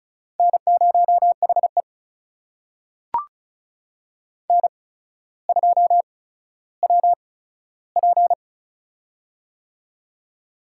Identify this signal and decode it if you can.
Morse code, heard as a keyed tone, about 36 wpm, spaced as for 15 wpm, 700 Hz. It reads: N0HE N2WP
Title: USA Call Signs - 4 Characters - 8x Character Spacing - 35wpm